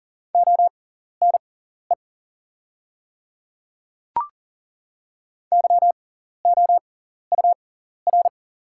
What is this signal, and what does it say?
Morse code, heard as a keyed tone, about 40 wpm, spaced as for 14 wpm, 690 Hz. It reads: ONE YOUR
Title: 6x Character Spacing - Top 100 Words - 40wpm